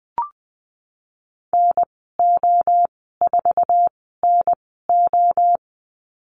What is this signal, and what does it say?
Morse code, heard as a keyed tone, about 20 wpm, spaced as for 15 wpm, 700 Hz. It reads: NO4NO
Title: 048 Lesson 9g - Number 4 - Call Signs 20wpm